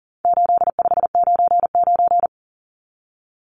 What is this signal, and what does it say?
Morse code, heard as a keyed tone, about 40 wpm, 700 Hz. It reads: 8599
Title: Numbers - 4-Digit 40wpm